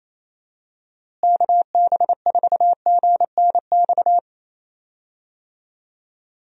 Morse code, heard as a keyed tone, about 27 wpm, 700 Hz. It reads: KB4GNX
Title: USA Call Signs - 6 Characters - Repeated 2x - 28wpm